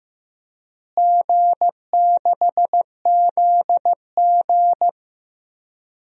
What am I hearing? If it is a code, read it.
Morse code, heard as a keyed tone, about 15 wpm, 700 Hz. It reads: G6ZG
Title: DXCC-100 Call Signs - 4 Characters - Repeated 3x - 15wpm